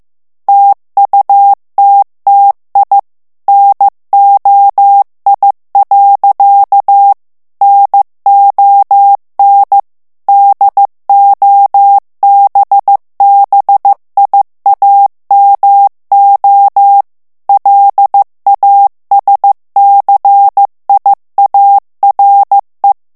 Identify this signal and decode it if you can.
Morse code, heard as a keyed tone, about 15 wpm, 800 Hz. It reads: TUTTI NOI. NON DOBBIAMO LASCIARE
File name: ITALIEN texte clair 1 vitesse 14 mots minute